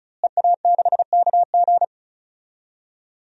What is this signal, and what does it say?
Morse code, heard as a keyed tone, about 36 wpm, 700 Hz. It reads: EA6KG
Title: DXCC-100 Call Signs - 5 Characters - Repeated 2x - 35wpm